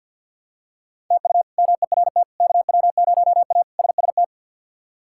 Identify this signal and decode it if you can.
Morse code, heard as a keyed tone, about 51 wpm, 700 Hz. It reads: TU MERT KW0A SST
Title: K1USN SST QSO Practice - 50wpm